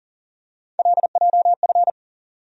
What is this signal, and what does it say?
Morse code, heard as a keyed tone, about 40 wpm, 700 Hz. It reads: LJF
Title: USA Call Sign Postfixes - 3 Letters 40wpm